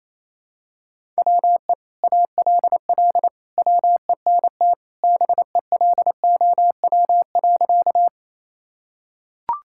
Morse code, heard as a keyed tone, about 28 wpm, 700 Hz. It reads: WE ALL WENT BELOW.
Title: Sentences from Top 1000 Words - Review Part 5 - 28wpm